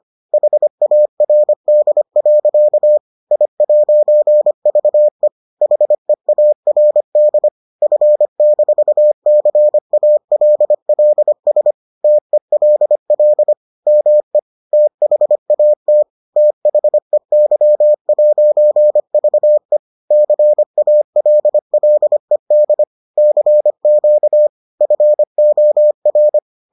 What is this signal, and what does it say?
Morse code, heard as a keyed tone, about 25 wpm, 600 Hz. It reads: HARD. I'VE HEARD F-CALLS TELL ME THAT THEY'VE CALLED CQ FOR